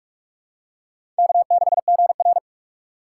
Morse code, heard as a keyed tone, about 42 wpm, 700 Hz. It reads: K6GR